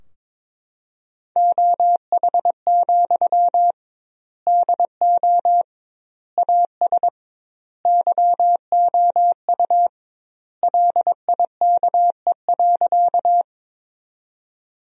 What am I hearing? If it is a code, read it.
Morse code, heard as a keyed tone, about 22 wpm, 700 Hz. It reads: OH, DO AS YOU LIKE.
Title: 2x Word-Spacing - Sentences from Top 500 Words - 22wpm